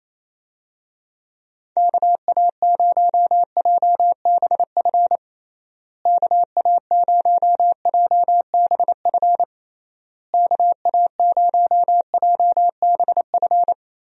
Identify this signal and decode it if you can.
Morse code, heard as a keyed tone, about 28 wpm, 700 Hz. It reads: KA0JBF KA0JBF KA0JBF
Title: USA Call Signs - 6 Characters - Repeated 3x - 28wpm